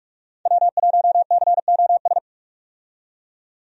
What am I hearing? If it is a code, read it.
Morse code, heard as a keyed tone, about 42 wpm, 700 Hz. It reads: W1COS
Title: USA Call Signs - 5 Characters - 45wpm